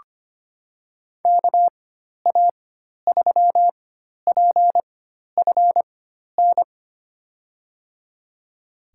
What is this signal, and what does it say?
Morse code, heard as a keyed tone, about 25 wpm, spaced as for 12 wpm, 700 Hz. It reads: KA3PFN
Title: USA Call Signs - 6 Characters - 4x Character Spacing - 25wpm